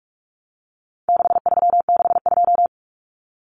Morse code, heard as a keyed tone, about 44 wpm, 700 Hz. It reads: 6362
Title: Numbers - 4-Digit 45wpm